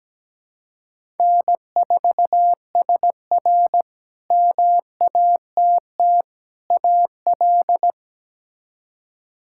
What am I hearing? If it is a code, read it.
Morse code, heard as a keyed tone, about 17 wpm, 700 Hz. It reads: N4SR MATT AL
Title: K1USN SST QSO Practice - 17wpm